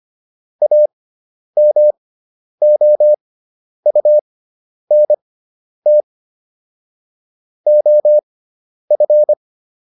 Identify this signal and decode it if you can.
Morse code, heard as a keyed tone, about 25 wpm, spaced as for 10 wpm, 600 Hz. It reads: AMOUNT OF